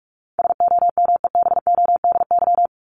Morse code, heard as a keyed tone, about 44 wpm, 700 Hz. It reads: SOMEBODY
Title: Top 1751-2000 Words 45wpm